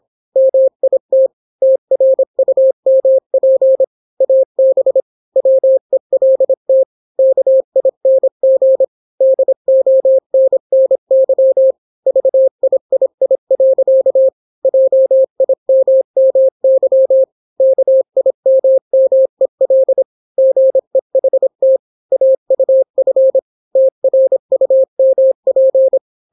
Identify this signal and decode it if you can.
Morse code, heard as a keyed tone, about 26 wpm, 530 Hz. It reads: MIT TRUMP AB WELT KING DONNY VIII. JIMMY KIMMEL GEHT AUF TRUMP